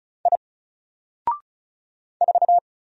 Morse code, heard as a keyed tone, about 34 wpm, 700 Hz. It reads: I 4
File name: Single-Letter-Number-35wpm